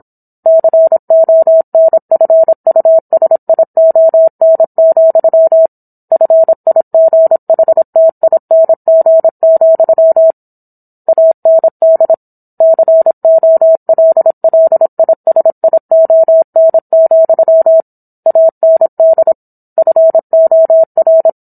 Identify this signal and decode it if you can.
Morse code, heard as a keyed tone, about 27 wpm, 650 Hz. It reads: CONFUSION, FIGHTING, AND COLLISION, AND FOR